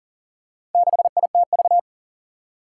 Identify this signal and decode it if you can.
Morse code, heard as a keyed tone, about 40 wpm, 700 Hz. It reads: 6ITV
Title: Four Challenging Letter-Number Combinations 40wpm